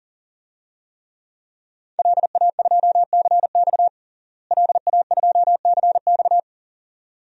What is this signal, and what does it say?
Morse code, heard as a keyed tone, about 37 wpm, 700 Hz. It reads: LA2CX LA2CX
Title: DXCC-100 Call Signs - 5 Characters - Repeated 2x - 40wpm